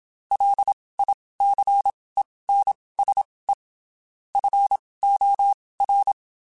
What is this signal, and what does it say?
Morse code, heard as a keyed tone, about 27 wpm, spaced as for 19 wpm, 800 Hz. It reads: LICENSE FOR
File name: (spaced) / QSOs Spaced-mp3s / kc7haa @25 sp